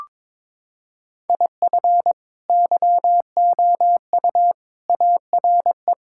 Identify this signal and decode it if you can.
Morse code, heard as a keyed tone, about 22 wpm, 700 Hz. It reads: IF YOU ARE
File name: practice / 22wpm / Sets of 8 Words - 22wpm